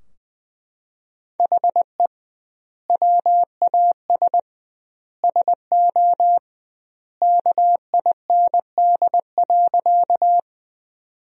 Morse code, heard as a keyed tone, about 21 wpm, 700 Hz. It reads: HE WAS SO KIND.
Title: 2x Word-Spacing - Sentences from Top 200 Words - 20wpm